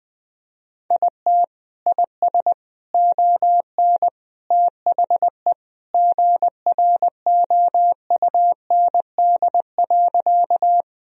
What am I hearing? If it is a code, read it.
Morse code, heard as a keyed tone, about 20 wpm, 700 Hz. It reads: IT IS ON THE GROUND.
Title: Sentences from Top 1000 Words - Review Part 2 - 20wpm